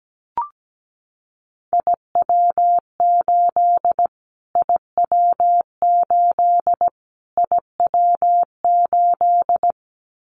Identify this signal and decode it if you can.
Morse code, heard as a keyed tone, about 17 wpm, 700 Hz. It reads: IW8 IW8 IW8
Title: Callsign Prefixes for Top 25 DXCC - Repeated 3x - 17wpm